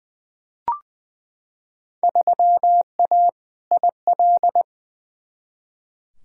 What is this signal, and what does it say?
Morse code, heard as a keyed tone, about 21 wpm, 700 Hz. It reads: 3A IL
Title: Field Day Exchanges - 20wpm